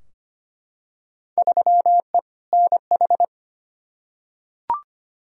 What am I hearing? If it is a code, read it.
Morse code, heard as a keyed tone, about 25 wpm, 700 Hz. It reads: 3E NH
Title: Field Day Exchanges Encore - 25wpm